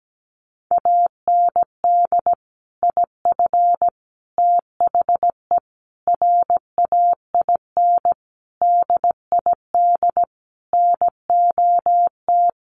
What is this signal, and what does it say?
Morse code, heard as a keyed tone, about 17 wpm, 700 Hz. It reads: AND IF THE RAIN DID NOT
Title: Sentences from Top 1300 Words 17wpm